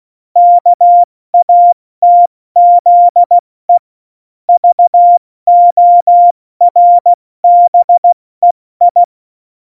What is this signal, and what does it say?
Morse code, heard as a keyed tone, about 16 wpm, 700 Hz. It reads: KATZE VORBEI
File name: Morsekurs / Witze / Witz 18. September 2018, 70 F80